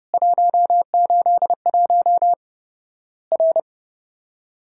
Morse code, heard as a keyed tone, about 31 wpm, 700 Hz. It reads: 181 R
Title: Worked All Europe DX Contest - QTC Exchange - 30wpm